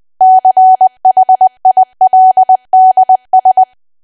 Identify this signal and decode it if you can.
Morse code, heard as a keyed tone, about 20 wpm, 750 Hz. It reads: CHILDS
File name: cwa / W1AW-Level3 / 20WPM